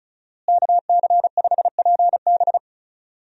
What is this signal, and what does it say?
Morse code, heard as a keyed tone, about 36 wpm, 700 Hz. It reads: KC5PB
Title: USA Call Signs - 5 Characters - Repeated 3x - 35wpm